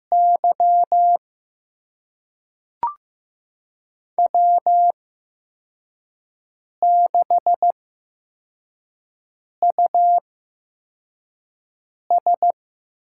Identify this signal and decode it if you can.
Morse code, heard as a keyed tone, about 15 wpm, spaced as for 4 wpm, 700 Hz. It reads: Y W6US
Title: USA Call Signs - 5 Characters - 8x Character Spacing - 15wpm